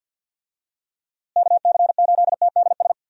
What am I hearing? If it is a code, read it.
Morse code, heard as a keyed tone, about 50 wpm, 700 Hz. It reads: KC8TDS
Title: USA Call Signs - 6 Characters - Repeated 3x - 50wpm